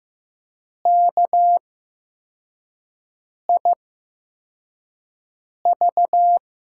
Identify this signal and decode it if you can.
Morse code, heard as a keyed tone, about 15 wpm, spaced as for 4 wpm, 700 Hz. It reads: KIV
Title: USA Call Signs - 5 Characters - 8x Character Spacing - 15wpm